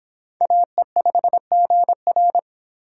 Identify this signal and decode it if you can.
Morse code, heard as a keyed tone, about 23 wpm, 700 Hz. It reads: AE5GR